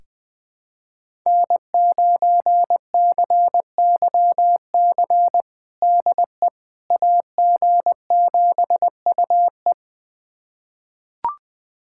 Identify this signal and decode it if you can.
Morse code, heard as a keyed tone, about 20 wpm, 700 Hz. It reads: N9CYC DE AG7UE
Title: USA Callsign DE Callsign - 5 Characters - 20wpm